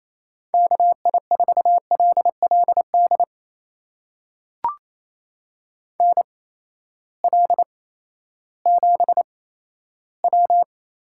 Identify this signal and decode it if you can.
Morse code, heard as a keyed tone, about 28 wpm, spaced as for 14 wpm, 700 Hz. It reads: KI4LLD NL7W
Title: USA Call Signs - 8x Character Spacing - 28wpm